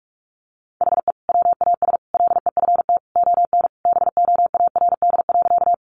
Morse code, heard as a keyed tone, about 44 wpm, 700 Hz. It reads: HE WAS LEFT ON BOARD.
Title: Sentences from Top 700 Words 45wpm